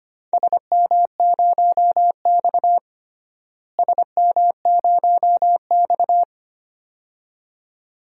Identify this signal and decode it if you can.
Morse code, heard as a keyed tone, about 25 wpm, 700 Hz. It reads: SM0X SM0X
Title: DXCC-100 Call Signs - 4 Characters - Repeated 2x - 25wpm